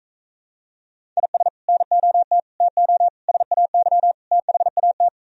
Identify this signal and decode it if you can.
Morse code, heard as a keyed tone, about 36 wpm, 700 Hz. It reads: IS NOT TO SAY THAT